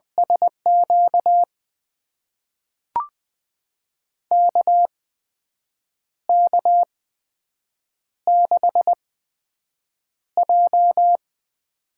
Morse code, heard as a keyed tone, about 20 wpm, spaced as for 7 wpm, 700 Hz. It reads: SQ KK6J
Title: USA Call Signs - 4 Characters - 8x Character Spacing - 20wpm